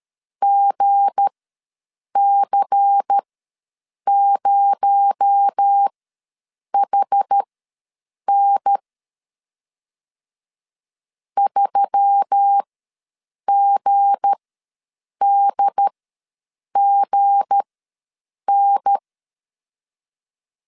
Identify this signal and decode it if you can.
Morse code, heard as a keyed tone, about 13 wpm, spaced as for 7 wpm, 800 Hz. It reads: GC0HN 3GDGN
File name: LEZIONI / 7 GF9Z / 7 G ws 12wpm 3140